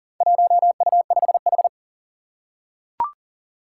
Morse code, heard as a keyed tone, about 38 wpm, 700 Hz. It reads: 1U5H
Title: Four Challenging Letter-Number Combinations 40wpm